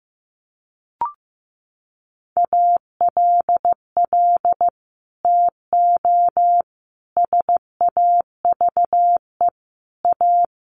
Morse code, heard as a keyed tone, about 15 wpm, 700 Hz. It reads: ALL TO SAVE A
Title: Sentences from Top 1100 Words 15wpm